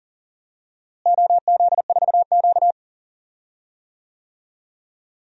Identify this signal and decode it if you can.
Morse code, heard as a keyed tone, about 40 wpm, 700 Hz. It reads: OZ4Q
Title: DXCC-100 Call Signs - 4 Characters - Repeated 2x - 40wpm